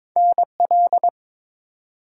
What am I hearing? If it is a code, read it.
Morse code, heard as a keyed tone, about 22 wpm, 700 Hz. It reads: NL
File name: practice / 22wpm / ARRL Sections v2 - 22wpm